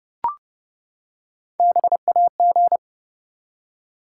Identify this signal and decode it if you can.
Morse code, heard as a keyed tone, about 30 wpm, 700 Hz. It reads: BAG
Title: Three Letter Words - 30wpm